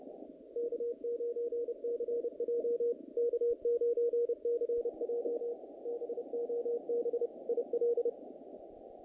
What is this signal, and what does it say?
Morse code, heard as a keyed tone, about 30 wpm, 470 Hz. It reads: K9CJ K9CJ BOB IL
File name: contests / naqp-cw-k5tr-op-k5pi- -2100